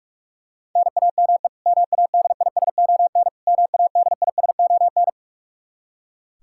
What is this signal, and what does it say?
Morse code, heard as a keyed tone, about 42 wpm, 700 Hz. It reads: NAME MADISON MADISON
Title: K1USN SST QSO Practice 45wpm